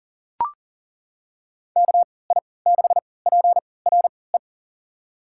Morse code, heard as a keyed tone, about 40 wpm, spaced as for 23 wpm, 700 Hz. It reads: KI6PRE
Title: USA Call Signs - 6 Characters - 3x Character Spacing - 40wpm